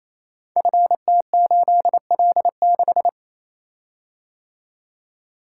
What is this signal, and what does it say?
Morse code, heard as a keyed tone, about 27 wpm, 700 Hz. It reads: FT8L6
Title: Code Groups - 5 Letter:Numbers - 28wpm